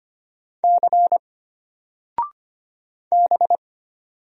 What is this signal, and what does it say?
Morse code, heard as a keyed tone, about 25 wpm, 700 Hz. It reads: C B
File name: Single-Letter-Number-25wpm